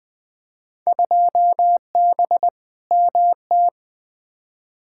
Morse code, heard as a keyed tone, about 20 wpm, 700 Hz. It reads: 2B MT
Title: Field Day Exchanges Encore - 20wpm